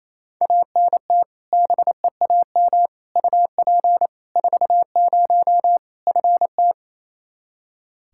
Morse code, heard as a keyed tone, about 28 wpm, 700 Hz. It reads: ANT BEAM UP 40 FT
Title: Practice QSO - Exchange 2 - Info - 28wpm